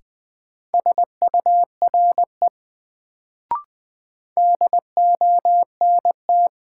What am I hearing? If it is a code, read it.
Morse code, heard as a keyed tone, about 20 wpm, 700 Hz. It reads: SURE DONT
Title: Sentences from Top 300 Words - New Word Drill - 20wpm